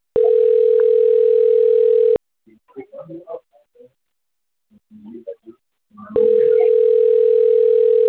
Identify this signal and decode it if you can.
Ringback tone (North America), 440 + 480 Hz.